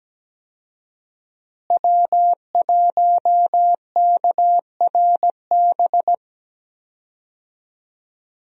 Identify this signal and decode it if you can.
Morse code, heard as a keyed tone, about 17 wpm, 700 Hz. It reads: W1KRB
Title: USA Call Signs - 5 Characters - Repeated 2x - 17wpm